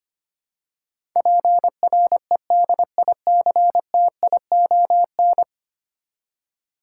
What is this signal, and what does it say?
Morse code, heard as a keyed tone, about 25 wpm, 690 Hz. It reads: PREDICTION
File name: 10 Letter Words - First 3 Letters Given - 25wpm